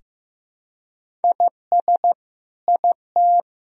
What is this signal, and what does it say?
Morse code, heard as a keyed tone, about 16 wpm, 700 Hz. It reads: IS IT
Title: Sentences from Top 1000 Words - Review Part 4 - 15wpm